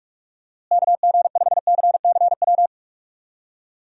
Morse code, heard as a keyed tone, about 46 wpm, 700 Hz. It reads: KG5CCW
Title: USA Call Signs - 6 Characters - Repeated 3x - 45wpm